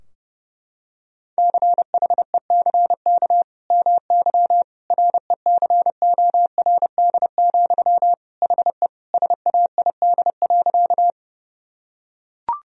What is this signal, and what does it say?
Morse code, heard as a keyed tone, about 32 wpm, 700 Hz. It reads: CHECK MY RECORD, HE SAID.
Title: Sentences from Top 1000 Words - Review Part 3 - 30wpm